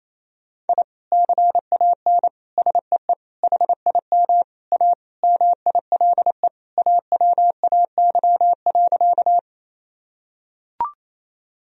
Morse code, heard as a keyed tone, about 28 wpm, 700 Hz. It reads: I CAN SEE HIM A MILE AWAY.
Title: Sentences from Top 1000 Words - Review Part 1 - 28wpm